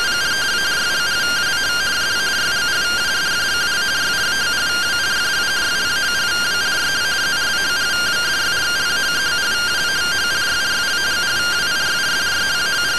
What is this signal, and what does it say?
Two-tone teleprinter signal, about 170 Hz shift.
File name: findings / MR AXIAL 33,99mm 1.gen-07 rtty rx